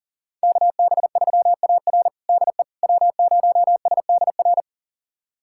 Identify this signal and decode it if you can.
Morse code, heard as a keyed tone, about 37 wpm, 700 Hz. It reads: KB3AR DE W0SDR
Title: USA Callsign DE Callsign - 5 Characters - 40wpm